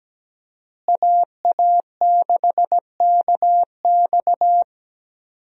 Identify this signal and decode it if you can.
Morse code, heard as a keyed tone, about 17 wpm, 700 Hz. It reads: AA6KX
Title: USA Call Signs - 5 Characters - Repeated 2x - 17wpm